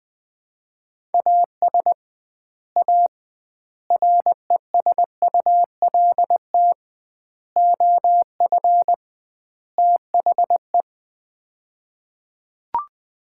Morse code, heard as a keyed tone, about 21 wpm, 700 Hz. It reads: AS A RESULT OF THE
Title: 2x Word-Spacing - Sets of 5 Words - 20wpm